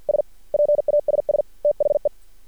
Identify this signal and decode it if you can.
Morse code, heard as a keyed tone, about 50 wpm, 590 Hz. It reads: S PASS THE
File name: archived / CRSnet